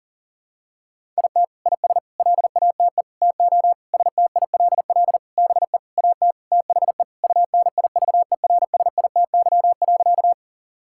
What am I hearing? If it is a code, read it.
Morse code, heard as a keyed tone, about 40 wpm, 700 Hz. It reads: IT IS LATE TO STILL BE AT THE UNIVERSITY.
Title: Sentences from Top 1000 Words - Review Part 5 - 40wpm